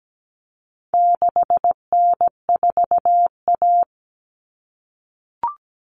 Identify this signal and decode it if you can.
Morse code, heard as a keyed tone, about 17 wpm, 700 Hz. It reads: 6N4A
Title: Common Cut Numbers - 4-Digit 17wpm